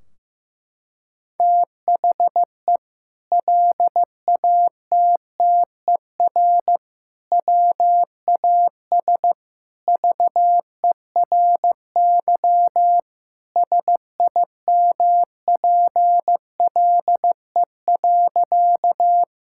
Morse code, heard as a keyed tone, about 15 wpm, 700 Hz. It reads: THE LATTER WAS VERY SIMPLE.
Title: Sentences from Top 1600 Words 15wpm